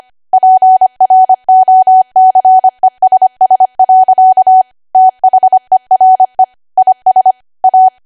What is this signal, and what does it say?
Morse code, heard as a keyed tone, about 25 wpm, 750 Hz. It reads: PROCESS. THERE IS A
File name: cwa / W1AW-Level3 / 25WPM